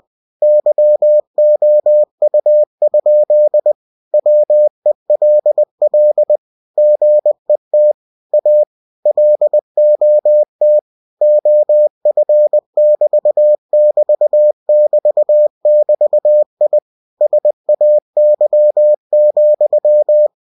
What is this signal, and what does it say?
Morse code, heard as a keyed tone, about 20 wpm, 600 Hz. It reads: YOU? WELL GET A LOT OF<BT><BT><BT><BT>I SAY,
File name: cw-books / winnie-the-pooh / winnie-20wpm-page-0037